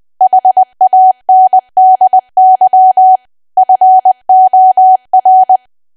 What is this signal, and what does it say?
Morse code, heard as a keyed tone, about 20 wpm, 750 Hz. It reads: HANDY FOR
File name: cwa / W1AW-Level3 / 20WPM